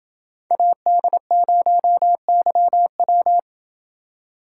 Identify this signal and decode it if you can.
Morse code, heard as a keyed tone, about 31 wpm, 700 Hz. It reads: AD0YW